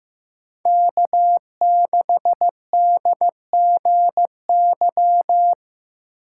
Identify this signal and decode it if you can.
Morse code, heard as a keyed tone, about 15 wpm, 700 Hz. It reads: K6DGY
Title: USA Call Signs - 2x Character Spacing - 15wpm